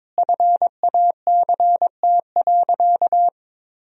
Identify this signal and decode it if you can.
Morse code, heard as a keyed tone, about 22 wpm, 700 Hz. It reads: FACT.
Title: 3x Word-Spacing - Sentences from Top 500 Words - 22wpm